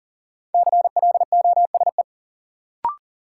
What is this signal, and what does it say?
Morse code, heard as a keyed tone, about 40 wpm, 700 Hz. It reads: CLOSE
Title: Sentences from Top 300 Words - New Word Drill - 40wpm